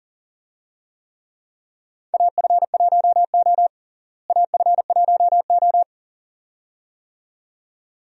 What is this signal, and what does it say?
Morse code, heard as a keyed tone, about 40 wpm, 700 Hz. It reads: AF1O AF1O
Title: USA Call Signs - 4 Characters - Repeated 2x - 40wpm